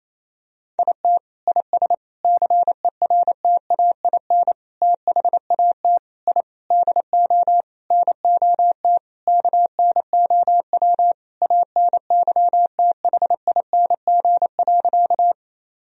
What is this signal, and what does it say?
Morse code, heard as a keyed tone, about 27 wpm, 700 Hz. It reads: IT IS CERTAIN THAT I DO NOT KNOW ANYTHING.
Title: Sentences from Top 1000 Words - Review Part 2 - 28wpm